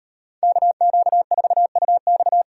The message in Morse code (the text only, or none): KQ4UX